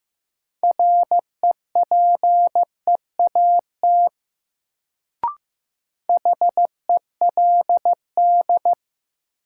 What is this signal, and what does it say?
Morse code, heard as a keyed tone, about 15 wpm, 700 Hz. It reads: REPEAT HELD
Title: Sentences from Top 1600 Words - New Word Drill Alt - 15wpm